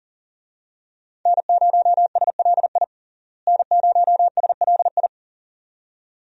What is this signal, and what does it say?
Morse code, heard as a keyed tone, about 40 wpm, 700 Hz. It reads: N0SLI N0SLI
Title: USA Call Signs - 5 Characters - Repeated 2x - 40wpm